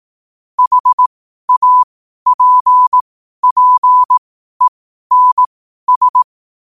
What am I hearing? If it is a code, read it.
Morse code, heard as a keyed tone, about 18 wpm, spaced as for 12 wpm, 1000 Hz. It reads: HAPPENS